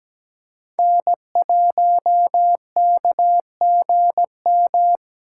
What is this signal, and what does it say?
Morse code, heard as a keyed tone, about 20 wpm, 700 Hz. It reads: N1KGM
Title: Speed Racing - USA 1x3 Call Signs 17wpm